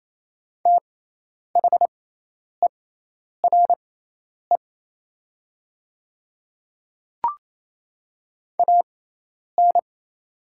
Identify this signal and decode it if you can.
Morse code, heard as a keyed tone, about 27 wpm, spaced as for 10 wpm, 700 Hz. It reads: THERE AN
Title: 6x Character Spacing - Top 200 Words - 28wpm